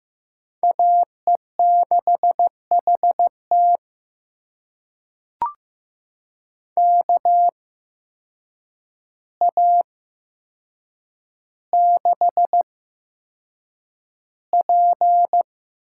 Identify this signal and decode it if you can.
Morse code, heard as a keyed tone, about 15 wpm, spaced as for 7 wpm, 700 Hz. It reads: AE6HT KA6P
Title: USA Call Signs - 8x Character Spacing - 15wpm